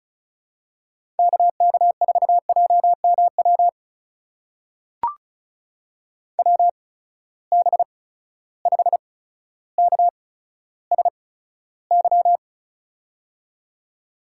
Morse code, heard as a keyed tone, about 36 wpm, spaced as for 15 wpm, 700 Hz. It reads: KK4JMW WB5KSY
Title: USA Call Signs - 6 Characters - 8x Character Spacing - 35wpm